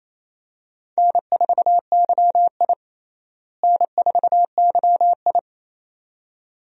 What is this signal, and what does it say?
Morse code, heard as a keyed tone, about 28 wpm, 700 Hz. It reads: N4YI N4YI
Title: USA Call Signs - 4 Characters - Repeated 2x - 28wpm